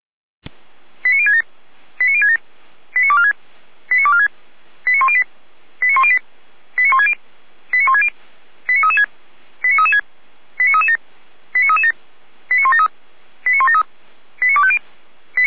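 Five-tone selective call (ZVEI), heard as quick runs of five tones, tones of about 70 ms. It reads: 89067 89067 89246 89246 89108 89108 89160 89160 89336 89336 89337 89337 89173 89173 89250
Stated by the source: ZVEI five-tone selective call received via SDR.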